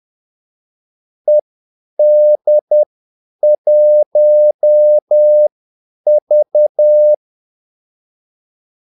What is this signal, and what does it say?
Morse code, heard as a keyed tone, about 10 wpm, spaced as for 8 wpm, 600 Hz. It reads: ED1V
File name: Callsigns-10-8wpm